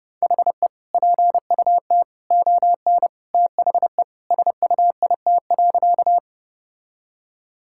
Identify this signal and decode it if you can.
Morse code, heard as a keyed tone, about 29 wpm, 700 Hz. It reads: HE PUT ON THE SUIT.
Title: Sentences from Top 1600 Words 30wpm